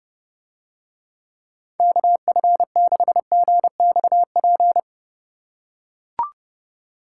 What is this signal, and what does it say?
Morse code, heard as a keyed tone, about 32 wpm, 700 Hz. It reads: KF6GXP